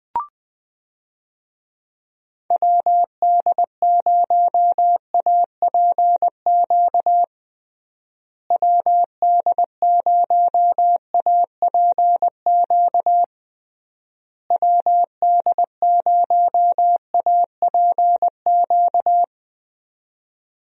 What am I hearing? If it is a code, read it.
Morse code, heard as a keyed tone, about 20 wpm, 700 Hz. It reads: WD0APQ WD0APQ WD0APQ
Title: USA Call Signs - 6 Characters - Repeated 3x - 20wpm